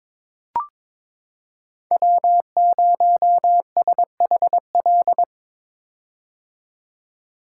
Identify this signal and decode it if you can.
Morse code, heard as a keyed tone, about 21 wpm, 700 Hz. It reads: W0SHL
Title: K1USN SST QSO Practice - 22wpm